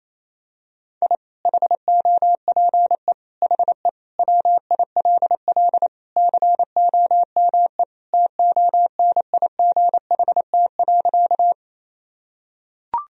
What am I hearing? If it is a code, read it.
Morse code, heard as a keyed tone, about 28 wpm, 700 Hz. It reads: I HOPE HE WILL COME TONIGHT.
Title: Sentences from Top 1000 Words - Review Part 3 - 28wpm